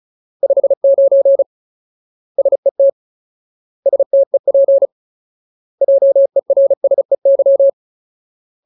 Morse code, heard as a keyed tone, about 34 wpm, 550 Hz. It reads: 59 SET STEP JERSEY
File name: Ultimate Word List / 35WPM Ultimate